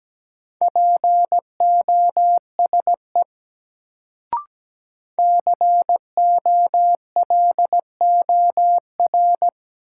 Morse code, heard as a keyed tone, about 17 wpm, 700 Hz. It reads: POSE COLOR
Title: Sentences from Top 1000 Words - Review Part 2 - New Word Drill - 17wpm